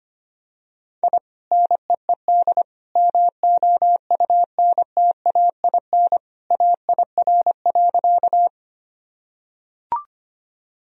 Morse code, heard as a keyed tone, about 25 wpm, 700 Hz. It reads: I NEED MOUNTAIN AIR.
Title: Sentences from Top 1000 Words - Review Part 1 - 25wpm